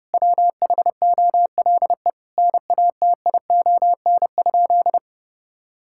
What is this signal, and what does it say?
Morse code, heard as a keyed tone, about 30 wpm, 700 Hz. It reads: WHOLE NATION?
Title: Sentences from Top 1000 Words - Review Part 6 - 30wpm